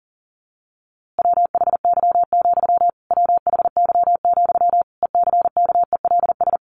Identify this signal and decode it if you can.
Morse code, heard as a keyed tone, about 40 wpm, 700 Hz. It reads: WHY, WHY, ECKELS
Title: A Sound Of Thunder 40wpm